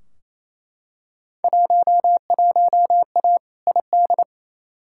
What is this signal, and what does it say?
Morse code, heard as a keyed tone, about 28 wpm, 700 Hz. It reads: 11A ID
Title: Field Day Exchanges Encore - 28wpm